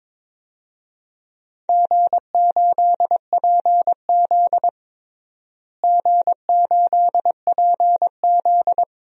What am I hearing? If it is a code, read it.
Morse code, heard as a keyed tone, about 22 wpm, 700 Hz. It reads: G8PZ G8PZ
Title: DXCC-100 Call Signs - 4 Characters - Repeated 2x - 22wpm